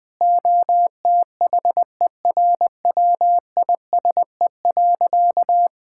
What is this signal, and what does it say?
Morse code, heard as a keyed tone, about 20 wpm, 700 Hz. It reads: OTHERWISE.
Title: Sentences with 6-Word Phrases - 20wpm